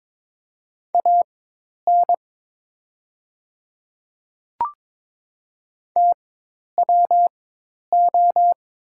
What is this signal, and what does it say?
Morse code, heard as a keyed tone, about 22 wpm, spaced as for 10 wpm, 700 Hz. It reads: AN TWO
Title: 4x Character Spacing - Top 200 Words - 22wpm